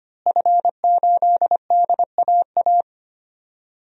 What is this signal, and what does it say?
Morse code, heard as a keyed tone, about 25 wpm, 700 Hz. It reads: F8DAA
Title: DXCC-100 Call Signs - 5 Characters - Repeated 3x - 25wpm